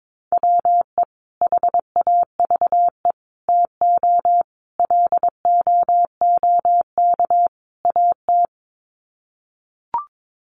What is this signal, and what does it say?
Morse code, heard as a keyed tone, about 22 wpm, 700 Hz. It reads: WE HAVE TO LOOK AT
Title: Sets of 5 Words Encore2 22wpm